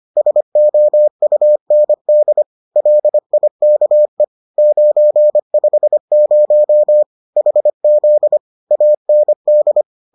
Morse code, heard as a keyed tone, about 25 wpm, 600 Hz. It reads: SOUND LIKE 950 HZ AND